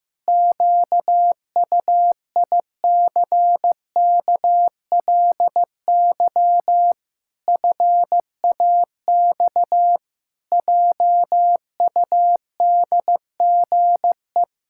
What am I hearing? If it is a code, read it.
Morse code, heard as a keyed tone, about 15 wpm, 700 Hz. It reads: QUICKLY FAX JUDGE
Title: Pangram Challenge v2 - 15wpm